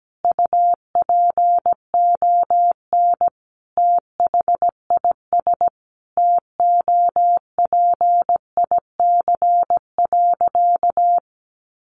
Morse code, heard as a keyed tone, about 17 wpm, 700 Hz. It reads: UPON THIS TOPIC.
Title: Sentences from Top 1200 Words 17wpm